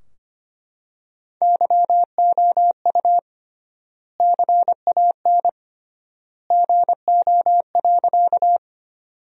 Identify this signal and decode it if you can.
Morse code, heard as a keyed tone, about 25 wpm, 700 Hz. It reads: YOU CAN GO.
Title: 3x Word-Spacing - Sentences from Top 100 Words - 25wpm